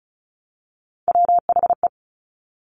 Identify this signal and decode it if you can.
Morse code, heard as a keyed tone, about 36 wpm, 700 Hz. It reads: WHE
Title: Top Three Letter Combinations in Words 35wpm